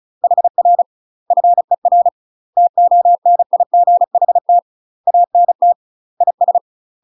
Morse code, heard as a keyed tone, about 34 wpm, 700 Hz. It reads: HR FER TONIGHT ANT IS
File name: UK MASTER QSOs TRAINING / 35WPM / UK MASTER QSOs 0037